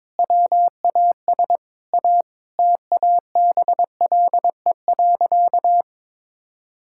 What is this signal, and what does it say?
Morse code, heard as a keyed tone, about 22 wpm, 700 Hz. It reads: WAS A TABLE.
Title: Sentences from Top 1000 Words - Review Part 6 - 22wpm